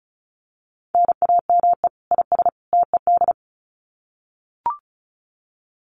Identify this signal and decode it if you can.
Morse code, heard as a keyed tone, about 34 wpm, 700 Hz. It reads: NAME IS TED
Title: Common QSO Phrases 35wpm